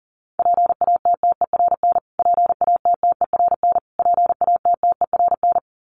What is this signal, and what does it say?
Morse code, heard as a keyed tone, about 40 wpm, 700 Hz. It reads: PATTERN PATTERN PATTERN
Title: Top 500 Words Repeated 40wpm